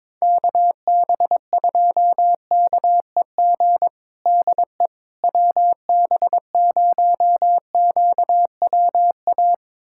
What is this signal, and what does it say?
Morse code, heard as a keyed tone, about 22 wpm, 700 Hz. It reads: KB2KEG DE WB0QWA
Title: USA Callsign DE Callsign 22wpm